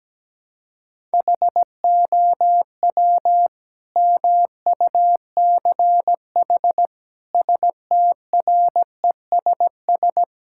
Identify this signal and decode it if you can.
Morse code, heard as a keyed tone, about 17 wpm, 700 Hz. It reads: HOW MUCH STRESS
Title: Sentences from Top 1000 Words - Review Part 6 - 17wpm